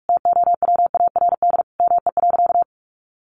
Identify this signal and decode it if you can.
Morse code, heard as a keyed tone, about 44 wpm, 700 Hz. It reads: TOWARD ME.
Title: Sentences from Top 1300 Words 45wpm